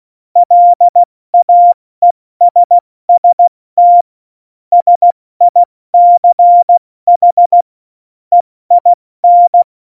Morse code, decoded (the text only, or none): LAESST SICH EIN